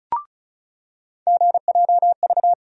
Morse code, heard as a keyed tone, about 36 wpm, 700 Hz. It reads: GJV
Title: USA Call Sign Postfixes - 3 Letters 35wpm